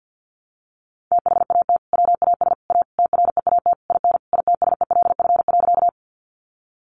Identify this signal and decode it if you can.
Morse code, heard as a keyed tone, about 50 wpm, 700 Hz. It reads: THAT WAS A TREAT IN ITSELF.
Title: Sentences from Top 1300 Words 50wpm